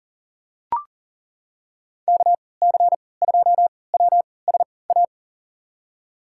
Morse code, heard as a keyed tone, about 40 wpm, spaced as for 23 wpm, 700 Hz. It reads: KC2WSA
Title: USA Call Signs - 6 Characters - 3x Character Spacing - 40wpm